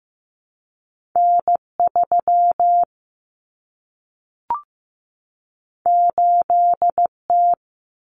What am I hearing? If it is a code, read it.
Morse code, heard as a keyed tone, about 15 wpm, 700 Hz. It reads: N3 8T
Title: Common Cut Numbers - 2-Digit 15wpm